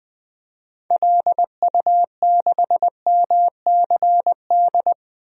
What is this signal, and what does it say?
Morse code, heard as a keyed tone, about 20 wpm, 700 Hz. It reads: LU6MCD
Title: DXCC-100 Call Signs - 6 Characters - 20wpm